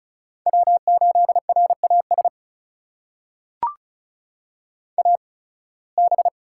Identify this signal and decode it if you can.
Morse code, heard as a keyed tone, about 34 wpm, spaced as for 23 wpm, 700 Hz. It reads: W8RAS AB
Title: USA Call Signs - 5 Characters - 8x Character Spacing - 35wpm